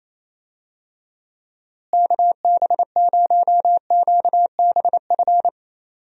Morse code, heard as a keyed tone, about 28 wpm, 700 Hz. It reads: KB0QBF
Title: USA Call Signs - 6 Characters - Repeated 2x - 28wpm